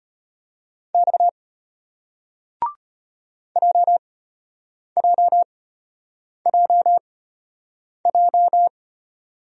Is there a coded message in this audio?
Yes: Morse code from 0.9 to 8.7 s, heard as a keyed tone, about 33 wpm, spaced as for 8 wpm, 700 Hz. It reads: X JJJJ